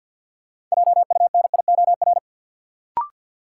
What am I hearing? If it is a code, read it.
Morse code, heard as a keyed tone, about 51 wpm, 700 Hz. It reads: JUNIOR
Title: Sentences from Top 1000 Words - Review Part 7 - New Word Drill - 50wpm